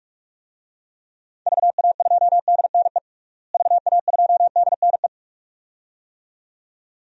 Morse code, heard as a keyed tone, about 42 wpm, 700 Hz. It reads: VA2DNE VA2DNE
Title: DXCC-100 Call Signs - 6 Characters - Repeated 2x - 45wpm